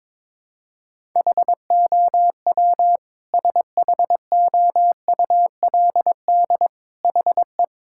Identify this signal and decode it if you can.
Morse code, heard as a keyed tone, about 22 wpm, 700 Hz. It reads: HOW SHOULD HE
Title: Sentences from Top 1000 Words - Review Part 6 - 22wpm